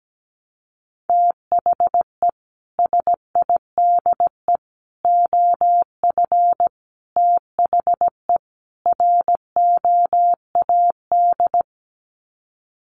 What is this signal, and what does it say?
Morse code, heard as a keyed tone, about 17 wpm, 700 Hz. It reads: THE SIDE OF THE ROAD
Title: Sets of 5 Words 17wpm